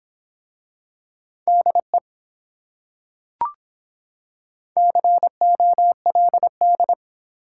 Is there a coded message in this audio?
Yes: Morse code from 1.5 to 6.9 s, heard as a keyed tone, about 24 wpm, 700 Hz. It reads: DE COLD